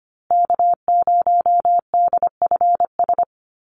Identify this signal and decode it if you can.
Morse code, heard as a keyed tone, about 25 wpm, 700 Hz. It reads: K0DFS
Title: USA Call Signs Repeated 3x 25wpm